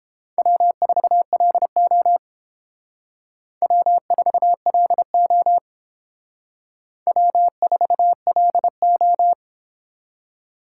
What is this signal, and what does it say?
Morse code, heard as a keyed tone, about 31 wpm, 700 Hz. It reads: W4LO W4LO W4LO